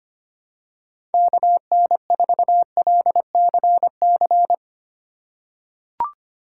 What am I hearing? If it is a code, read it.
Morse code, heard as a keyed tone, about 25 wpm, 700 Hz. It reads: KN4LCC
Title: USA Call Signs - 3x Character Spacing - 25wpm